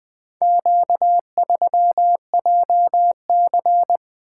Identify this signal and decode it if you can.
Morse code, heard as a keyed tone, about 20 wpm, 700 Hz. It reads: Q3JC
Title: Four Letter-Number Combinations - 20wpm